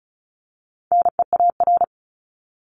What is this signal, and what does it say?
Morse code, heard as a keyed tone, about 34 wpm, 700 Hz. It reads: NEAR
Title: Top 200 Words Repeated 35wpm